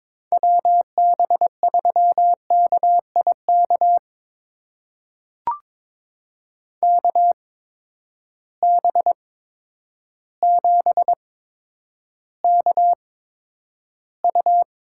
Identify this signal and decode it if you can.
Morse code, heard as a keyed tone, about 22 wpm, spaced as for 10 wpm, 700 Hz. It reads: WB3KIK KB7KU
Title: USA Call Signs - 6 Characters - 8x Character Spacing - 22wpm